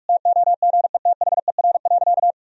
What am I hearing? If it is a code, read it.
Morse code, heard as a keyed tone, about 46 wpm, 700 Hz. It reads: TOGETHER.
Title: Sentences from Top 400 Words and Abbreviations 45wpm